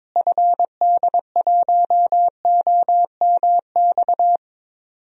Morse code, heard as a keyed tone, about 21 wpm, 700 Hz. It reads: FD1OMX
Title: DXCC-100 Call Signs - 6 Characters - Repeated 2x - 22wpm